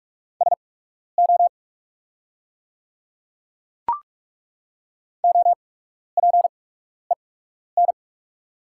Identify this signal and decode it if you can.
Morse code, heard as a keyed tone, about 43 wpm, spaced as for 12 wpm, 690 Hz. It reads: SO OPEN
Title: 8x Character Spacing - Top 300 Words - 45wpm